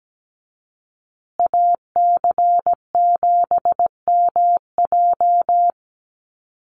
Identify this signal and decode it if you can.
Morse code, heard as a keyed tone, about 17 wpm, 700 Hz. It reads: AC7MJ
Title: USA Call Signs Repeated 2x 17wpm